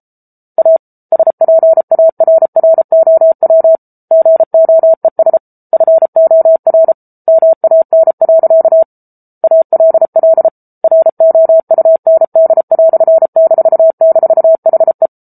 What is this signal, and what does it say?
Morse code, heard as a keyed tone, about 34 wpm, 650 Hz. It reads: A SPARROW GOES FOR MAN. ALL ROUND"--HE